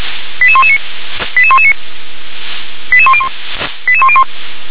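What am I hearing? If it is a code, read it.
Five-tone selective call (ZVEI), heard as quick runs of five tones, tones of about 70 ms. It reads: 88119 88119 88101 88101
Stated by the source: ZVEI five-tone selective call received on SDR.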